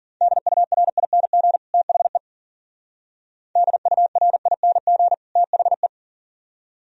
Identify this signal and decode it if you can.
Morse code, heard as a keyed tone, about 49 wpm, 700 Hz. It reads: DURING THE DURING THE
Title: Speed-Racing - Sets of 2 Words 40wpm